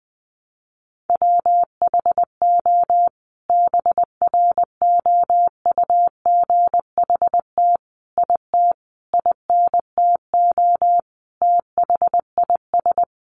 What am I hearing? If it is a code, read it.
Morse code, heard as a keyed tone, about 20 wpm, 700 Hz. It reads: WHO BROUGHT IT INTO THIS
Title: Sentences from Top 600 Words 20wpm